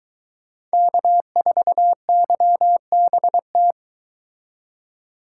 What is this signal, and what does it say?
Morse code, heard as a keyed tone, about 22 wpm, 700 Hz. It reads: K4YBT